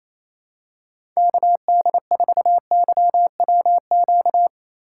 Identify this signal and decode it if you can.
Morse code, heard as a keyed tone, about 28 wpm, 700 Hz. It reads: KD4YWQ
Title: USA Call Signs - 6 Characters - 3x Character Spacing - 28wpm